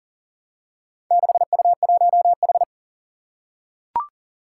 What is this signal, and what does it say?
Morse code, heard as a keyed tone, about 39 wpm, 700 Hz. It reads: 6U1H
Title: Four Challenging Letter-Number Combinations 40wpm